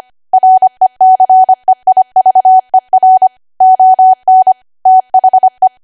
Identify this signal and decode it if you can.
Morse code, heard as a keyed tone, about 25 wpm, 750 Hz. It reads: RECEIVER ON THE